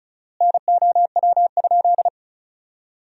Morse code, heard as a keyed tone, about 34 wpm, 700 Hz. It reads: NOW?
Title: 4x Word-Spacing - Sentences from Top 200 Words - 35wpm